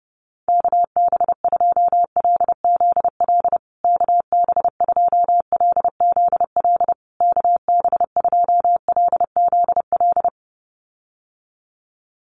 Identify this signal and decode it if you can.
Morse code, heard as a keyed tone, about 30 wpm, 700 Hz. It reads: KB2LZL KB2LZL KB2LZL
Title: USA Call Signs Repeated 3x 30wpm